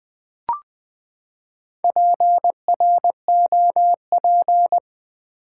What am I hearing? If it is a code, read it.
Morse code, heard as a keyed tone, about 20 wpm, 700 Hz. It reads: PROP
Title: Build A Word - 4 Letters At A Time - 10-Letter Words - 20wpm